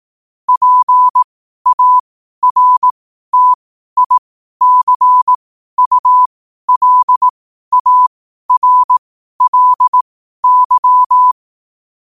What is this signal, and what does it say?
Morse code, heard as a keyed tone, about 18 wpm, spaced as for 12 wpm, 1000 Hz. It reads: PARTICULARLY
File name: qotdfiles / 12wpm